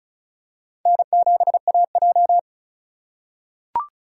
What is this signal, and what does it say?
Morse code, heard as a keyed tone, about 36 wpm, 700 Hz. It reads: N7AJ
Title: USA Call Signs - 3x Character Spacing - 35wpm